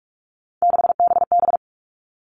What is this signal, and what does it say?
Morse code, heard as a keyed tone, about 44 wpm, 700 Hz. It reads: 6BB
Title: All Cut Numbers - 3-Digit 45wpm